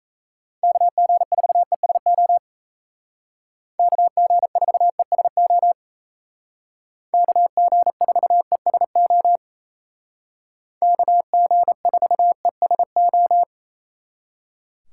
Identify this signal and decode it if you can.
Morse code, heard as a keyed tone, about 36 wpm, 700 Hz. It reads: KG4ESO KG4ESO KG4ESO KG4ESO